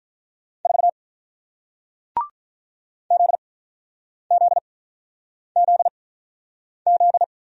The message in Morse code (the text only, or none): 4 ZZZZ